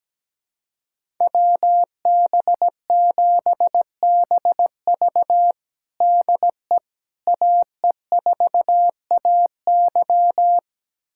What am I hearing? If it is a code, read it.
Morse code, heard as a keyed tone, about 17 wpm, 700 Hz. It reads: WB7BV DE AE4AY
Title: USA Callsign DE Callsign - 5 Characters - 17wpm